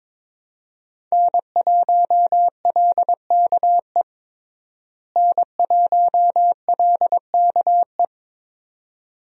Morse code, heard as a keyed tone, about 22 wpm, 700 Hz. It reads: N1LKE N1LKE
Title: USA Call Signs - 5 Characters - Repeated 2x - 22wpm